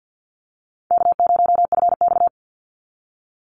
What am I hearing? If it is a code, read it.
Morse code, heard as a keyed tone, about 50 wpm, 700 Hz. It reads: K0FX
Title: Easy Call Signs - 50wpm